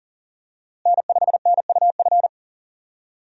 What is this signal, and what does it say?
Morse code, heard as a keyed tone, about 40 wpm, 700 Hz. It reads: N5NUF
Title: USA Call Signs - 5 Characters - Repeated 3x - 40wpm